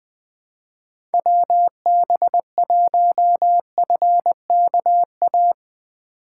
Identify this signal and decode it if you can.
Morse code, heard as a keyed tone, about 22 wpm, 700 Hz. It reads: WB1FKA
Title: Speed Racing - USA Call Signs 20wpm